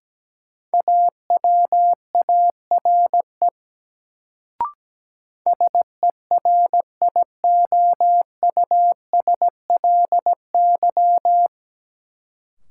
Morse code, heard as a keyed tone, about 17 wpm, 700 Hz. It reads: AWARE SERIOUSLY
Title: Sentences from Top 1200 Words - New Word Drill Alt - 17wpm